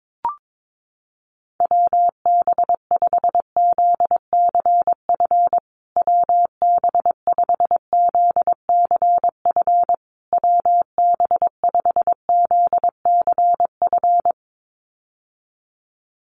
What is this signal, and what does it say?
Morse code, heard as a keyed tone, about 22 wpm, 700 Hz. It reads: WB5ZCF WB5ZCF WB5ZCF
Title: USA Call Signs Repeated 3x 22wpm